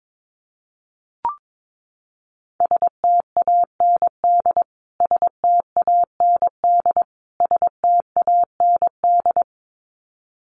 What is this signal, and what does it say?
Morse code, heard as a keyed tone, about 22 wpm, 700 Hz. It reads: STAND STAND STAND
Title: Top 500 Words Repeated 22wpm